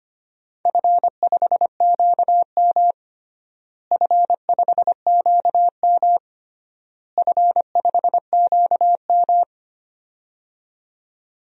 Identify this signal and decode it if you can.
Morse code, heard as a keyed tone, about 25 wpm, 700 Hz. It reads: F5QM F5QM F5QM
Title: DXCC-100 Call Signs - 4 Characters - Repeated 3x - 25wpm